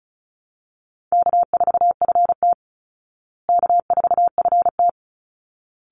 Morse code, heard as a keyed tone, about 34 wpm, 700 Hz. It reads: K4FT K4FT
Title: Call Signs Repeated 3x - 35wpm